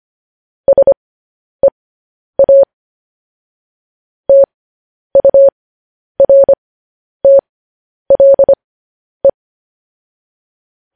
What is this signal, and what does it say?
Morse code, heard as a keyed tone, about 25 wpm, spaced as for 10 wpm, 550 Hz. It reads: SEA TURTLE
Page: news-headlines-in-morse-code-at-10-wpm